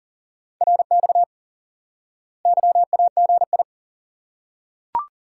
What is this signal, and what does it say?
Morse code, heard as a keyed tone, about 40 wpm, 700 Hz. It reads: RX YAGI
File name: IQE Course 3 - Lesson 89 - YAGI - Two Elements - 40wpm